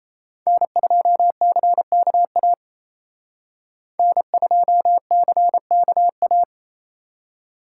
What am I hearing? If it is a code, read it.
Morse code, heard as a keyed tone, about 35 wpm, 700 Hz. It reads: N2CKA N2CKA
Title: Speed Racing - USA Call Signs 28wpm